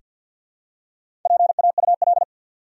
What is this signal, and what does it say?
Morse code, heard as a keyed tone, about 48 wpm, 700 Hz. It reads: PAUL
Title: USA Common Names - Max 5 Letters - 50wpm